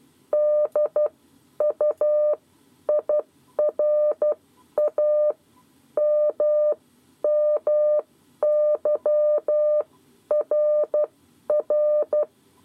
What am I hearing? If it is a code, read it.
Morse code, heard as a keyed tone, about 12 wpm, spaced as for 10 wpm, 600 Hz. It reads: DUIRA MMYRR
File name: 7 Lezione cw 2